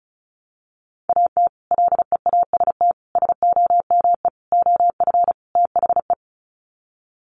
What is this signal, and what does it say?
Morse code, heard as a keyed tone, about 34 wpm, 700 Hz. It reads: AT LEAST SOME OF THE
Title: Sets of 5 Words Encore 35wpm